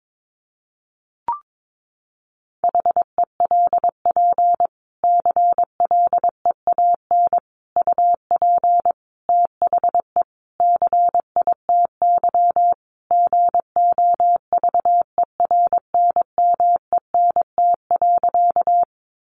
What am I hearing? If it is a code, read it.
Morse code, heard as a keyed tone, about 22 wpm, 700 Hz. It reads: HELP CLEAN UP THE CITY GOVERNMENT.
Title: Sentences from Top 1100 Words 22wpm